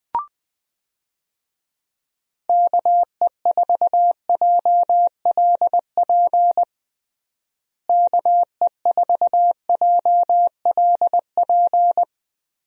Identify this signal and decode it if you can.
Morse code, heard as a keyed tone, about 21 wpm, 700 Hz. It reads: KE4JLP KE4JLP
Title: USA Call Signs - 6 Characters - Repeated 3x - 20wpm